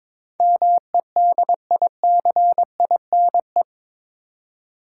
Morse code, heard as a keyed tone, about 22 wpm, 700 Hz. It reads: MEDICINE
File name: practice / 22wpm / Eight Letter Words - 22wpm